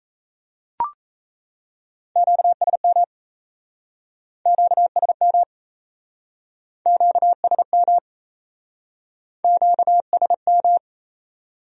Morse code, heard as a keyed tone, about 39 wpm, 700 Hz. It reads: QSM QSM QSM QSM